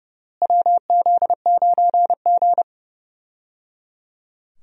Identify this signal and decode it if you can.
Morse code, heard as a keyed tone, about 30 wpm, 700 Hz. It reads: WZ9G
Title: USA Call Signs - 4 Characters - Repeated 3x - 30wpm